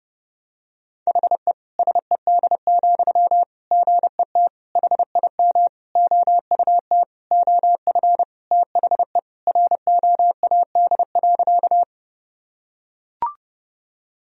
Morse code, heard as a keyed tone, about 30 wpm, 700 Hz. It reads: HE SED, GET HIM OUT OF THE ROAD.
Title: Sentences from Top 400 Words and Abbreviations 30wpm